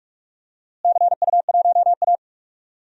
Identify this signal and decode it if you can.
Morse code, heard as a keyed tone, about 42 wpm, 700 Hz. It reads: CU1A
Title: DXCC-100 Call Signs - 4 Characters - Repeated 2x - 45wpm